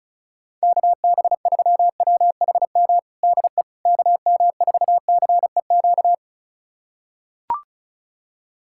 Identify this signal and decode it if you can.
Morse code, heard as a keyed tone, about 34 wpm, 700 Hz. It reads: KB3WHM DE KM4CEQ
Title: USA Callsign DE Callsign - 6 Characters - 35wpm